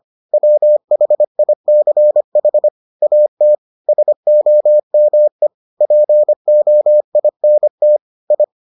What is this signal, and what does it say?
Morse code, heard as a keyed tone, about 25 wpm, 600 Hz. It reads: WHICH AT SOME POINT I